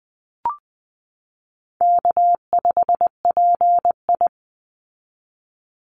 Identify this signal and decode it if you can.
Morse code, heard as a keyed tone, about 20 wpm, 700 Hz. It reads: K5PI
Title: Easy Call Signs Repeated 3x - 20wpm